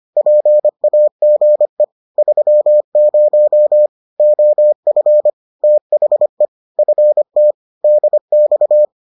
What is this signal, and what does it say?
Morse code, heard as a keyed tone, about 25 wpm, 600 Hz. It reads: PAGE 30 OF THE FT DX